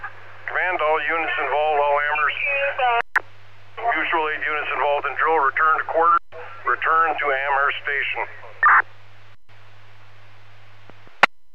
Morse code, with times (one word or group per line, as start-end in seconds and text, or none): none